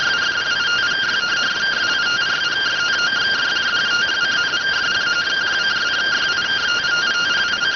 Two-tone teleprinter signal, about 170 Hz shift.